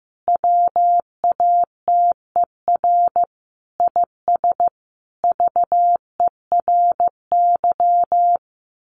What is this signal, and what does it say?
Morse code, heard as a keyed tone, about 14 wpm, 700 Hz. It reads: WATER IS VERY
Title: Sentences from Top 1500 Words 15wpm